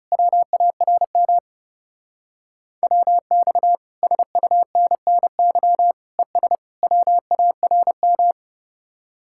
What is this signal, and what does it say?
Morse code, heard as a keyed tone, about 36 wpm, 700 Hz. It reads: WARM WX SUNNY ES WARM
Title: Speed-Racing Common QSO Phrases Encore - 30wpm